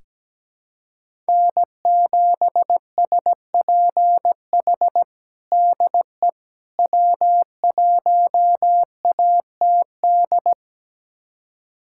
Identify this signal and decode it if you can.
Morse code, heard as a keyed tone, about 17 wpm, 700 Hz. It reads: N7SPH DE W1ATD
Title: USA Callsign DE Callsign - 5 Characters - 17wpm